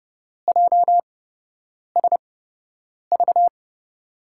Morse code, heard as a keyed tone, about 31 wpm, spaced as for 8 wpm, 700 Hz. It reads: JSV
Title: USA Call Signs - 6 Characters - 8x Character Spacing - 30wpm